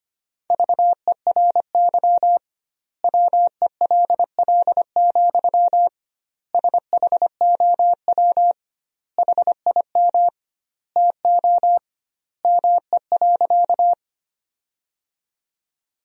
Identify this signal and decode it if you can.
Morse code, heard as a keyed tone, about 25 wpm, 700 Hz. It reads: VERY WELL, SHOW HIM TO ME.
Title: 2x Word-Spacing - Sentences from Top 200 Words - 25wpm